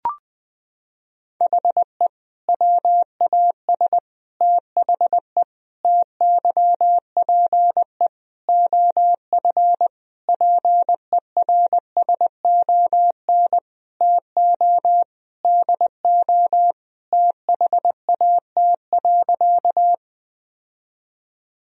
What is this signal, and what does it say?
Morse code, heard as a keyed tone, about 20 wpm, 700 Hz. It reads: HE WAS THE TYPE OF PERSON TO DO THAT.
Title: Sentences from Top 1000 Words - Review Part 3 - 20wpm